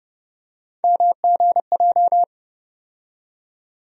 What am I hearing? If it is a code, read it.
Morse code, heard as a keyed tone, about 30 wpm, 700 Hz. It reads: MGJ
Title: USA Call Sign Postfixes - 3 Letters 30wpm